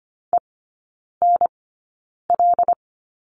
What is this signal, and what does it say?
Morse code, heard as a keyed tone, about 25 wpm, spaced as for 9 wpm, 700 Hz. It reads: ENL
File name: course / 25:10wpm / 093 Lesson 16e - Letter c - Three Characters 25wpm